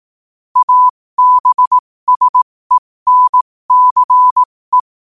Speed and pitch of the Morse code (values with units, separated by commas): 18 wpm, spaced as for 15 wpm, 1000 Hz